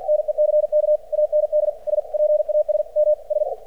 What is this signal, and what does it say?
Morse code, heard as a keyed tone, about 50 wpm, 600 Hz. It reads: NECK AND S PAS M S
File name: archived / CRSnet